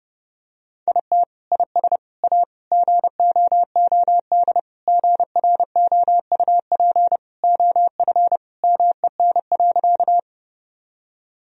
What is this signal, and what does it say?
Morse code, heard as a keyed tone, about 30 wpm, 700 Hz. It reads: IT IS A GOOD GROUP OF MEN.
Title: Sentences from Top 1000 Words - Review Part 1 - 30wpm